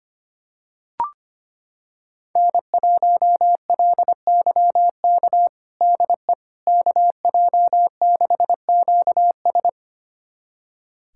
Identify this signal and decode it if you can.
Morse code, heard as a keyed tone, about 25 wpm, 700 Hz. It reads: N1LYK DE KJ6QS
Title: USA Callsign DE Callsign - 5 Characters - 25wpm